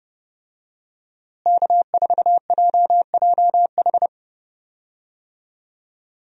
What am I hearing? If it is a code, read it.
Morse code, heard as a keyed tone, about 30 wpm, 700 Hz. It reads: K4JJH
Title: USA Call Signs - 5 Characters - Repeated 2x - 30wpm